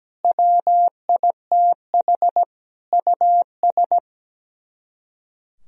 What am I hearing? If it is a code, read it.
Morse code, heard as a keyed tone, about 17 wpm, 700 Hz. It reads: WITH US
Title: Speed-Racing - Sets of 5 Words 17wpm